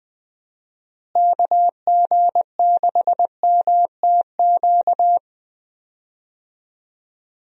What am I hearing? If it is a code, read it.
Morse code, heard as a keyed tone, about 20 wpm, 700 Hz. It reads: KG6MTQ
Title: USA Call Signs - 6 Characters - Repeated 3x - 20wpm